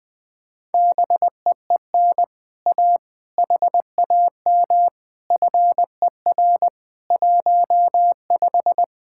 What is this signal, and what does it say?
Morse code, heard as a keyed tone, about 20 wpm, 700 Hz. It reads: BEEN A HAM FER 15
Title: Speed-Racing Common QSO Phrases - 20wpm